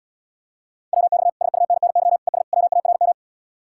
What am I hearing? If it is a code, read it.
Morse code, heard as a keyed tone, about 78 wpm, 700 Hz. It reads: WX SUNNY ES WARM